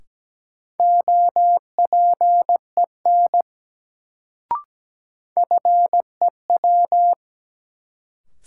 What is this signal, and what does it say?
Morse code, heard as a keyed tone, about 17 wpm, 700 Hz. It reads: OPEN FEW
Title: Sentences from Top 300 Words - New Word Drill - 17wpm